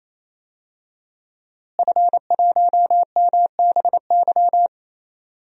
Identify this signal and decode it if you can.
Morse code, heard as a keyed tone, about 27 wpm, 700 Hz. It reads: F1MBY
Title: DXCC-100 Call Signs - 5 Characters - Repeated 3x - 28wpm